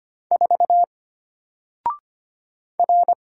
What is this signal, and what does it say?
Morse code, heard as a keyed tone, about 25 wpm, 700 Hz. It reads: 4 R
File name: Single-Letter-Number-25wpm